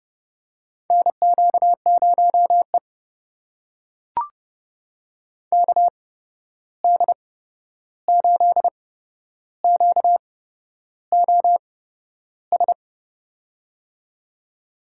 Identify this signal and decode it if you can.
Morse code, heard as a keyed tone, about 32 wpm, spaced as for 11 wpm, 700 Hz. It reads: NQ0E KD8QOS
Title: USA Call Signs - 8x Character Spacing - 30wpm